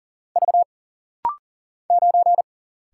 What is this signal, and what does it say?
Morse code, heard as a keyed tone, about 39 wpm, 700 Hz. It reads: V 9
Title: Single Letter-Number Alt - Rapid-Fire 40wpm